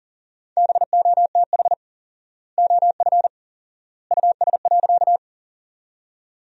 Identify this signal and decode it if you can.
Morse code, heard as a keyed tone, about 40 wpm, 700 Hz. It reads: BOTH OF US.
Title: 4x Word-Spacing - Sentences from Top 300 Words - 40wpm